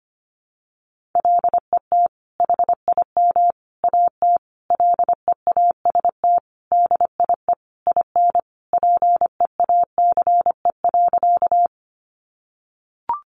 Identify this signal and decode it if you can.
Morse code, heard as a keyed tone, about 25 wpm, 700 Hz. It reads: LET HIM AT LEAST DIE IN PEACE.
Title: Sentences from Top 1500 Words 25wpm